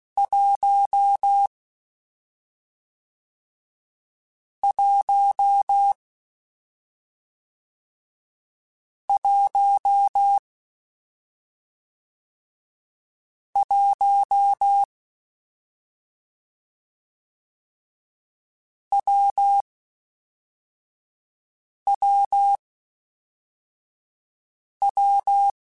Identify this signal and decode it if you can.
Morse code, heard as a keyed tone, about 16 wpm, 800 Hz. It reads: 1 1 1 1 W W W
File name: Difficult Combos / Difficult Combos-mp3s / WPJ WPJ1 @5